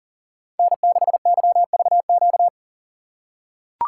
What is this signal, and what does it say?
Morse code, heard as a keyed tone, about 40 wpm, 700 Hz. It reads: N6YVQ
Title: USA Call Signs - 5 Characters - 4x Character Spacing - 40wpm